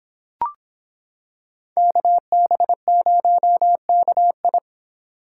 Morse code, heard as a keyed tone, about 26 wpm, 700 Hz. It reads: KB0KI